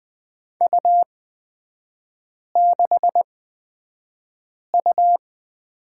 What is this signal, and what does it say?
Morse code, heard as a keyed tone, about 20 wpm, spaced as for 5 wpm, 690 Hz. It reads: U6U
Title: 229 Lesson 37e - Letter x - Three Characters 20wpm